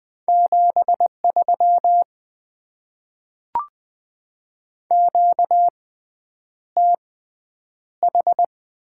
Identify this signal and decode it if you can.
Morse code, heard as a keyed tone, about 20 wpm, spaced as for 9 wpm, 690 Hz. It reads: 73 QTH
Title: Top 20 QSO Elements - 6x Character Spacing - 20wpm